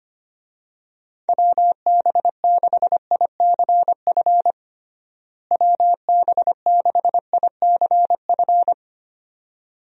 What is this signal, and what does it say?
Morse code, heard as a keyed tone, about 25 wpm, 700 Hz. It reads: WB6ICF WB6ICF
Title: USA Call Signs - 6 Characters - Repeated 2x - 25wpm